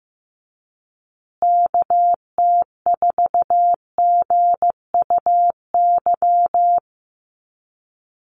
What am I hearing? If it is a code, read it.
Morse code, heard as a keyed tone, about 15 wpm, 700 Hz. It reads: KT4GUY
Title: USA Call Signs Repeated 2x 15wpm